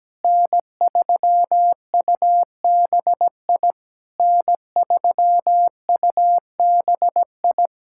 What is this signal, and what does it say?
Morse code, heard as a keyed tone, about 17 wpm, 700 Hz. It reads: N3UBI N3UBI
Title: Speed-Racing Common QSO Phrases - 17wpm